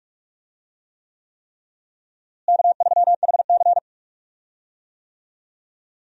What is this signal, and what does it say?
Morse code, heard as a keyed tone, about 41 wpm, 700 Hz. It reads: K3HC
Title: USA Call Signs - 4 Characters - Repeated 3x - 45wpm